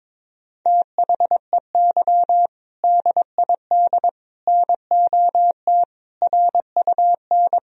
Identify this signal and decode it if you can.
Morse code, heard as a keyed tone, about 22 wpm, 700 Hz. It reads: THEY DID NOT RUN
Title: Sentences from Top 1000 Words - Review Part 1 - 22wpm